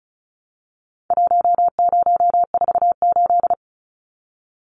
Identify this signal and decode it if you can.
Morse code, heard as a keyed tone, about 34 wpm, 700 Hz. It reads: 1048
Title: Numbers - Contest Style - 4-Digit 35wpm